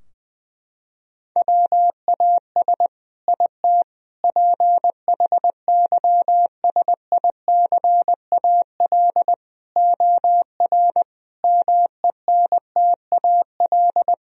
Morse code, heard as a keyed tone, about 20 wpm, 700 Hz. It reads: WAS IT PHYSICAL OR MENTAL
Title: Sentences from Top 1000 Words - Review Part 4 - 20wpm